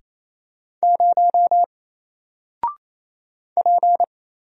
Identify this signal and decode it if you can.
Morse code, heard as a keyed tone, about 27 wpm, 700 Hz. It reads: 0 P
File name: Single-Letter-Number-28wpm